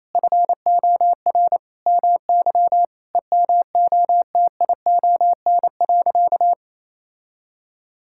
Morse code, heard as a keyed tone, about 28 wpm, 700 Hz. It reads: FOR MY EMOTION.
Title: Sentences from Top 1600 Words 28wpm